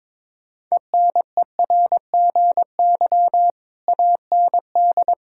Morse code, heard as a keyed tone, about 22 wpm, 700 Hz. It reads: ENERGY AND
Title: Sentences from Top 1000 Words - Review Part 6 - 22wpm